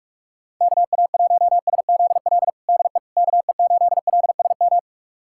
Morse code, heard as a keyed tone, about 46 wpm, 700 Hz. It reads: KA1SZL DE KE8LSM
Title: USA Callsign DE Callsign - 6 Characters - 45wpm